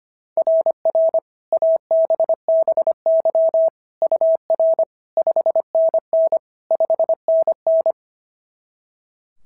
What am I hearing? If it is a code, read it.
Morse code, heard as a keyed tone, about 25 wpm, 690 Hz. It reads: RR ABBY UR 5NN 5NN
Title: SOTA QSO Practice 25wpm